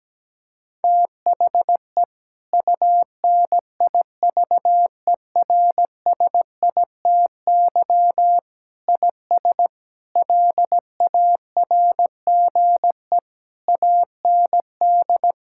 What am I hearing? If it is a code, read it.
Morse code, heard as a keyed tone, about 17 wpm, 700 Hz. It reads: THE UNIVERSITY IS LARGE AND
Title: Sentences from Top 1000 Words - Review Part 5 - 17wpm